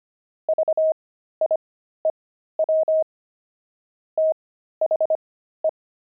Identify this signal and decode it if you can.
Morse code, heard as a keyed tone, about 25 wpm, spaced as for 13 wpm, 630 Hz. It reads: VIEW THE
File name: pangram-fs-13-cs-25-dexterity